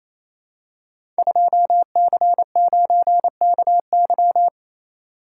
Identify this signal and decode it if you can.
Morse code, heard as a keyed tone, about 28 wpm, 700 Hz. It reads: 2C9KY
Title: Code Groups - 5 Letter:Numbers - 28wpm